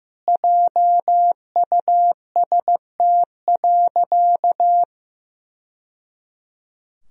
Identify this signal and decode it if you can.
Morse code, heard as a keyed tone, about 15 wpm, 700 Hz. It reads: JUST.
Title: Sentences from Top 1000 Words - Review Part 1 - 15wpm